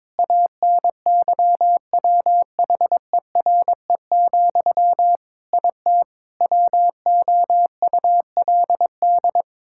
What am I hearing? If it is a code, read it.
Morse code, heard as a keyed tone, about 22 wpm, 700 Hz. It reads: ANYWHERE, IT WOULD
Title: Sentences from Top 1600 Words 22wpm